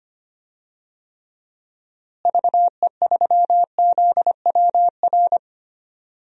Morse code, heard as a keyed tone, about 25 wpm, 700 Hz. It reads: VE3ZWR
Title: DXCC-100 Call Signs - 6 Characters - Repeated 3x - 25wpm